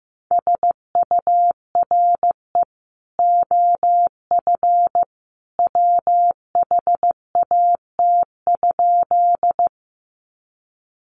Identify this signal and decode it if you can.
Morse code, heard as a keyed tone, about 15 wpm, 700 Hz. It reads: SURE OF WHAT?
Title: Sentences from Top 300 Words 15wpm